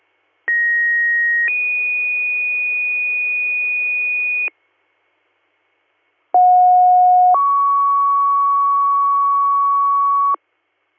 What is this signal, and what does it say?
Two-tone page, heard as a one-second tone, then a three-second tone, about 1900 then 2450 Hz, about 730 then 1120 Hz. It is heard from 0.5 to 10.4 s.